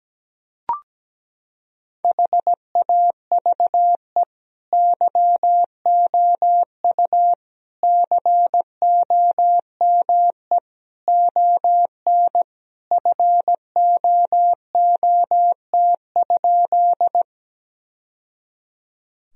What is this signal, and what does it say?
Morse code, heard as a keyed tone, about 17 wpm, 700 Hz. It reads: HAVE YOU COME ON FOOT?
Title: Sentences from Top 1000 Words - Review Part 3 - 17wpm